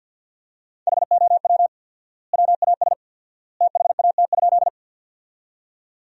Morse code, heard as a keyed tone, about 52 wpm, 700 Hz. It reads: HOW WAS THAT?
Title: 4x Word-Spacing - Sentences from Top 100 Words - 50wpm